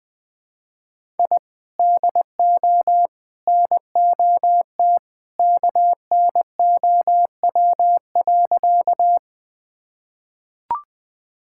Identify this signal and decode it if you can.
Morse code, heard as a keyed tone, about 20 wpm, 700 Hz. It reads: I DO NOT KNOW.
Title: Sentences with 3-Word Phrases - Spoken Beginning - 20wpm